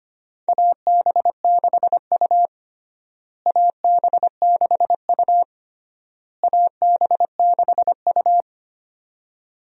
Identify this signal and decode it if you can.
Morse code, heard as a keyed tone, about 25 wpm, 700 Hz. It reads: AB6U AB6U AB6U
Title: USA Call Signs - 4 Characters - Repeated 3x - 25wpm